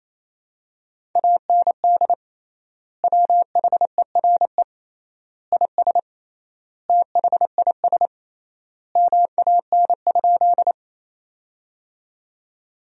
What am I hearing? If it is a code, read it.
Morse code, heard as a keyed tone, about 27 wpm, 700 Hz. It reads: AND WHERE IS THIS MAN?
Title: 3x Word-Spacing - Sentences from Top 200 Words - 28wpm